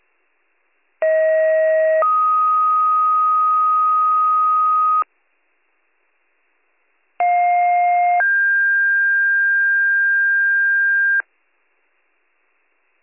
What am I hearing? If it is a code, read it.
Two-tone page, heard as a one-second tone, then a three-second tone, about 640 then 1250 Hz, about 710 then 1600 Hz. It is heard from 1.0 to 11.2 s.